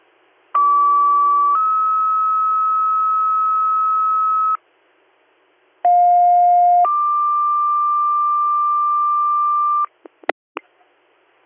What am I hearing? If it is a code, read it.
Two-tone page, heard as a one-second tone, then a three-second tone, about 1150 then 1290 Hz, about 710 then 1180 Hz. It is heard from 0.5 to 9.9 s.